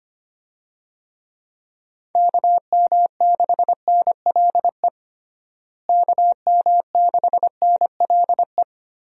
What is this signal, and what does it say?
Morse code, heard as a keyed tone, about 25 wpm, 700 Hz. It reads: KM6NLE KM6NLE
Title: USA Call Signs - 6 Characters - Repeated 3x - 25wpm